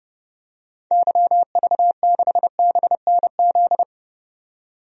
Morse code, heard as a keyed tone, about 31 wpm, 700 Hz. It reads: YV6BNZ
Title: DXCC-100 Call Signs - 6 Characters - 30wpm